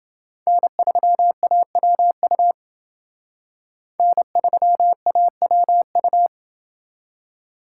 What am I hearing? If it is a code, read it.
Morse code, heard as a keyed tone, about 28 wpm, 700 Hz. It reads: N3AWU N3AWU